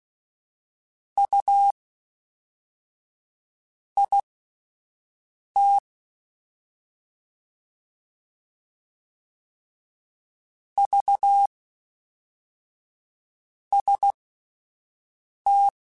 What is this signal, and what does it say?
Morse code, heard as a keyed tone, about 16 wpm, spaced as for 4 wpm, 800 Hz. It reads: UIT VST
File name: CW Comparisons / CW Comparisons-mp3s / CW Comparisons Broken Characters @10